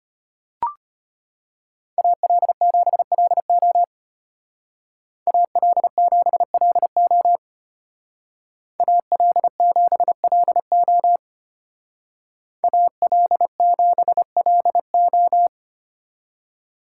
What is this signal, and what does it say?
Morse code, heard as a keyed tone, about 33 wpm, 700 Hz. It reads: AL7LO AL7LO AL7LO AL7LO